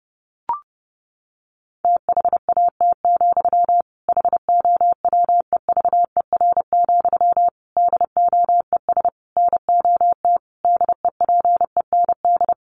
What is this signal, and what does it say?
Morse code, heard as a keyed tone, about 30 wpm, 700 Hz. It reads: THAT, HOWEVER, DOES NOT DEPEND
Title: Sentences from Top 1300 Words 30wpm